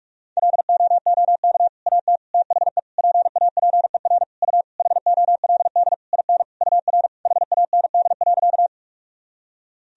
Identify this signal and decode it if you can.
Morse code, heard as a keyed tone, about 46 wpm, 700 Hz. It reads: LOOK AT THE PAPER U HOLD IN UR HAND.
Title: Sentences from Top 400 Words and Abbreviations 45wpm